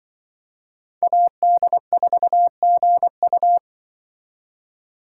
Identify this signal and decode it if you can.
Morse code, heard as a keyed tone, about 25 wpm, 700 Hz. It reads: AD4GU